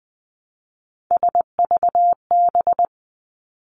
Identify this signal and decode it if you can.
Morse code, heard as a keyed tone, about 20 wpm, 700 Hz. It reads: SVB
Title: Three Challenging Letter-Number Combinations 20wpm